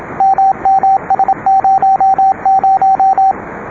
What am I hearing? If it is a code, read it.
Morse code, heard as a keyed tone, about 26 wpm, 780 Hz, with radio noise. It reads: MMS00
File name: rs-12 bake